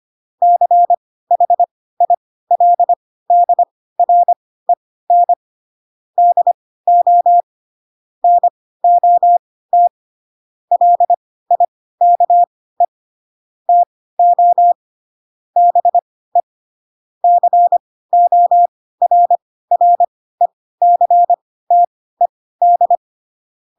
Morse code, heard as a keyed tone, about 25 wpm, spaced as for 16 wpm, 700 Hz. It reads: CHILDREN DO NOT LIKE TO BE CORRECTED
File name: Pinocchio / Pinocchio ch4 10 25 ik1hge